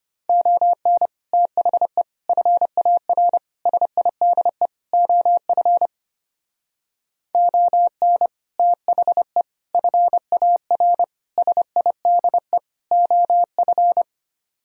Morse code, heard as a keyed tone, about 33 wpm, 700 Hz. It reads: ON THE FAR SIDE OF ON THE FAR SIDE OF